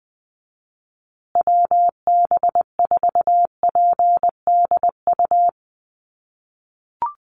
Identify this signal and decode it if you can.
Morse code, heard as a keyed tone, about 20 wpm, 700 Hz. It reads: WB4PDU
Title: USA Call Signs Repeated 2x 20wpm